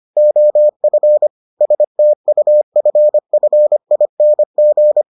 Morse code, heard as a keyed tone, about 25 wpm, 600 Hz. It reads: OF STUFFING